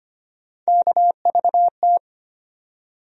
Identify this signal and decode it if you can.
Morse code, heard as a keyed tone, about 25 wpm, 700 Hz. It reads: KVT
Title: USA Call Sign Postfixes - 3 Letters 25wpm